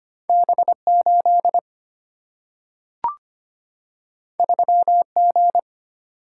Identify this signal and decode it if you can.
Morse code, heard as a keyed tone, about 25 wpm, 700 Hz. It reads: B8 3G
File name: Two Letter-Number Combinations - 25wpm